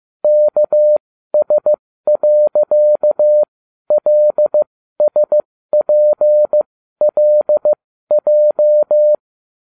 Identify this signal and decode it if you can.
Morse code, heard as a keyed tone, about 15 wpm, spaced as for 12 wpm, 600 Hz. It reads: KS.LSPLJ